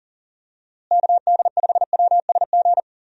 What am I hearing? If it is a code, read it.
Morse code, heard as a keyed tone, about 40 wpm, 700 Hz. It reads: KD5WSG
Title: USA Call Signs - 6 Characters - Repeated 3x - 40wpm